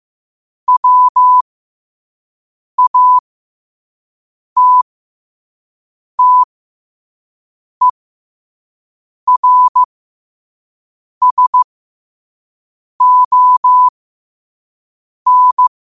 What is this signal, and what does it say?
Morse code, heard as a keyed tone, about 15 wpm, spaced as for 5 wpm, 1000 Hz. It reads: WATTERSON
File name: qotdfiles / 5wpm